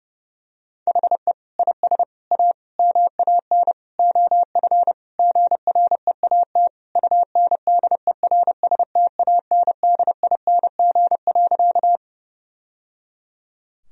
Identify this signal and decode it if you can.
Morse code, heard as a keyed tone, about 30 wpm, 700 Hz. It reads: HE IS A MAN OF GREAT UNDERSTANDING.
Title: Sentences from Top 1000 Words - Review Part 3 - 30wpm